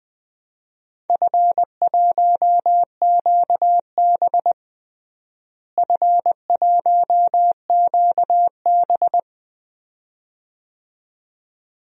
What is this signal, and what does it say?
Morse code, heard as a keyed tone, about 20 wpm, 700 Hz. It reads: F1QB F1QB
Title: DXCC-100 Call Signs - 4 Characters - Repeated 2x - 20wpm